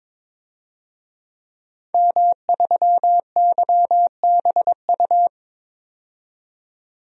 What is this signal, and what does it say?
Morse code, heard as a keyed tone, about 22 wpm, 700 Hz. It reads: M3YBU
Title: DXCC-100 Call Signs - 5 Characters - 22wpm